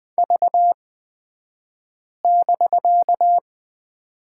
Morse code, heard as a keyed tone, about 20 wpm, spaced as for 5 wpm, 700 Hz. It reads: V<BK>
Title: 249 Lesson 40d - BK - Two Characters 20wpm